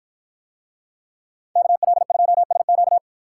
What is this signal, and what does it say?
Morse code, heard as a keyed tone, about 50 wpm, 700 Hz. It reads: KL2SQ